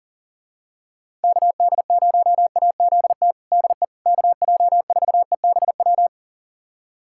Morse code, heard as a keyed tone, about 40 wpm, 700 Hz. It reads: KD0AZT DE KJ4EBW
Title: USA Callsign DE Callsign 40wpm